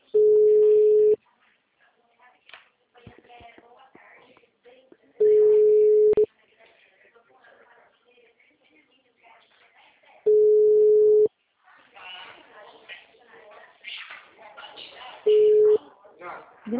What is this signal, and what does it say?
Ringback tone (Europe, 425 Hz), with speech.